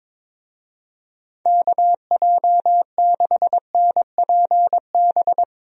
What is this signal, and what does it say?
Morse code, heard as a keyed tone, about 22 wpm, 700 Hz. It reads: KJ6NPB
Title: USA Call Signs - 6 Characters - Repeated 3x - 22wpm